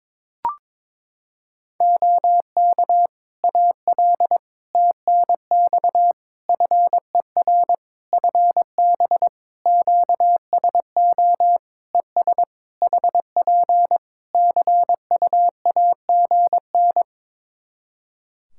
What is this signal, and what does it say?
Morse code, heard as a keyed tone, about 23 wpm, 700 Hz. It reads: OK AL TNX FER FB QSO ES HP CUAGN
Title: Practice QSO - Exchange 4 - Ending - 22wpm